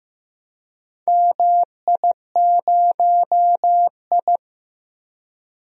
Morse code, heard as a keyed tone, about 15 wpm, 700 Hz. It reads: MI0I
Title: DXCC-100 Call Signs - 4 Characters - 15wpm